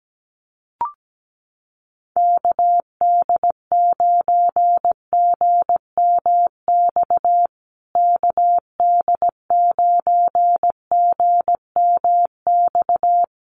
Morse code, heard as a keyed tone, about 17 wpm, 700 Hz. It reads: KD9GMX KD9GMX
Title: USA Call Signs Repeated 3x 17wpm